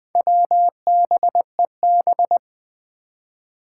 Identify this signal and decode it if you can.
Morse code, heard as a keyed tone, about 20 wpm, 700 Hz. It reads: WBEB
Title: Four Challenging Letter-Number Combinations 20wpm